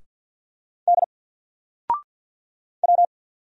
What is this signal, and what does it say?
Morse code, heard as a keyed tone, about 50 wpm, 700 Hz. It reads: D W
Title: Single Letter-Number 50wpm